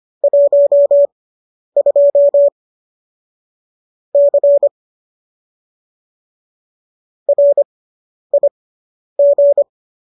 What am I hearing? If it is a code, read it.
Morse code, heard as a keyed tone, about 25 wpm, spaced as for 10 wpm, 570 Hz. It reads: 12 C RIG